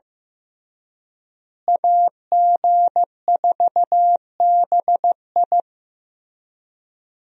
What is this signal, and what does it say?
Morse code, heard as a keyed tone, about 15 wpm, 700 Hz. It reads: AG4BI
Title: USA Call Signs - 5 Characters - Repeated 3x - 15wpm